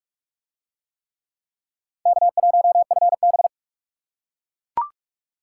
Morse code, heard as a keyed tone, about 39 wpm, 700 Hz. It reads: K1FB